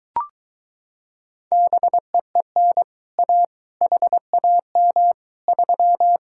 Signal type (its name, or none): Morse code